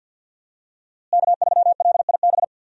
Morse code, heard as a keyed tone, about 53 wpm, 700 Hz. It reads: K3LIB
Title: USA Call Signs - 5 Characters - Repeated 3x - 50wpm